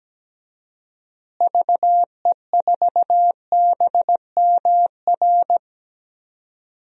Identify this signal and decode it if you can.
Morse code, heard as a keyed tone, about 17 wpm, 700 Hz. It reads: VE4BMR
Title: DXCC-100 Call Signs - 6 Characters - Repeated 3x - 17wpm